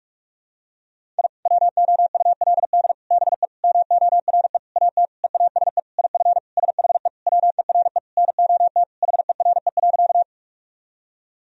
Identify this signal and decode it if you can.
Morse code, heard as a keyed tone, about 41 wpm, 700 Hz. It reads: I WOULD BE MORE AT EASE IF SHE WERE NOT HERE.
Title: Sentences from Top 1000 Words - Review Part 1 - 45wpm